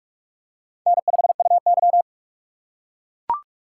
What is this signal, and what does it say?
Morse code, heard as a keyed tone, about 44 wpm, 690 Hz. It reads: N5UY
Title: USA Call Signs - 4x Character Spacing - 45wpm